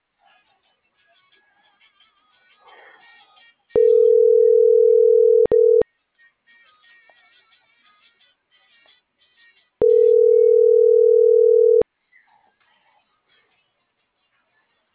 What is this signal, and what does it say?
Ringback tone (North America), 440 + 480 Hz.